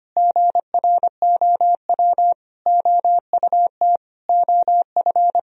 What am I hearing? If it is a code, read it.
Morse code, heard as a keyed tone, about 25 wpm, 700 Hz. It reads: GROW OUT OF
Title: Sentences from Top 1000 Words - Review Part 1 - 25wpm